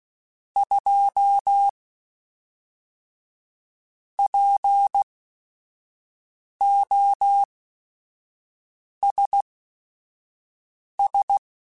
Morse code, heard as a keyed tone, about 16 wpm, spaced as for 5 wpm, 800 Hz. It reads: 2 POSS
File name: Technical Subjects / Technical Subjects-mp3s / AND Gate @5